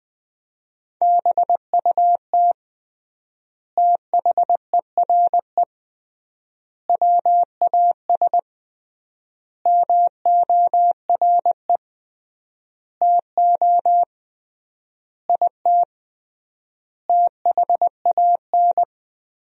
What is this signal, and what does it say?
Morse code, heard as a keyed tone, about 20 wpm, 700 Hz. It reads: BUT THERE WAS MORE TO IT THAN
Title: 3x Word-Spacing - Sentences from Top 100 Words - 20wpm